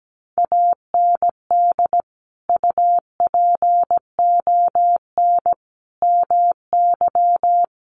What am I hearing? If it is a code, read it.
Morse code, heard as a keyed tone, about 17 wpm, 700 Hz. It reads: AND UPON MY
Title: Sentences from Top 1100 Words 17wpm